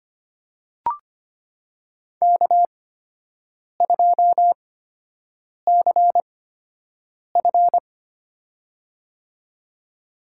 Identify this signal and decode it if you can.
Morse code, heard as a keyed tone, about 25 wpm, spaced as for 7 wpm, 700 Hz. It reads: K2CF
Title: USA Call Signs - 8x Character Spacing - 25wpm